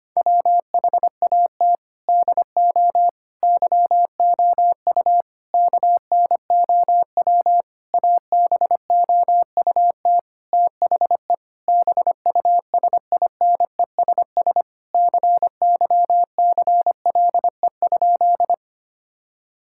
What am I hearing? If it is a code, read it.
Morse code, heard as a keyed tone, about 25 wpm, 700 Hz. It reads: WHAT DO YOU KNOW ABOUT THE BUSINESS CYCLE?
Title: Sentences from Top 1000 Words - Review Part 6 - 25wpm